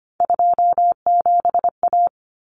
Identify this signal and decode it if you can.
Morse code, heard as a keyed tone, about 25 wpm, 700 Hz. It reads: 27A
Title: Common Cut Numbers - 3-Digit 25wpm